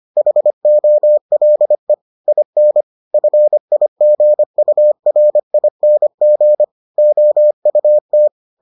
Morse code, heard as a keyed tone, about 25 wpm, 600 Hz. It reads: HOLE IN FIGURING OUT